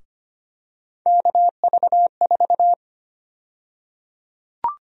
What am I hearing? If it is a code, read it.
Morse code, heard as a keyed tone, about 25 wpm, 700 Hz. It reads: KV4
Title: USA Call Sign Prefixes - Repeated 2x - 25wpm